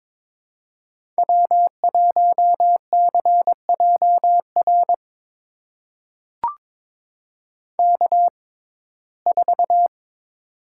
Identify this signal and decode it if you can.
Morse code, heard as a keyed tone, about 23 wpm, spaced as for 16 wpm, 700 Hz. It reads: W1CJR K4
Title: USA Call Signs - 5 Characters - 6x Character Spacing - 22wpm